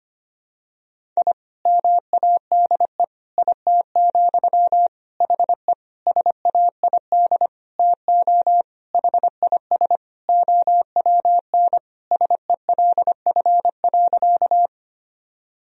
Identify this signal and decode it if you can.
Morse code, heard as a keyed tone, about 25 wpm, 700 Hz. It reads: I MADE IT, HE SAID TO HIS OWN SELF.
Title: Sentences from Top 1000 Words - Review Part 1 - 25wpm